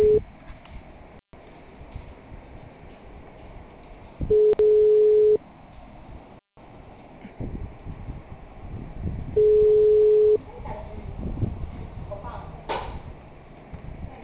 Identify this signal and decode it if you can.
Ringback tone (Europe, 425 Hz), with speech.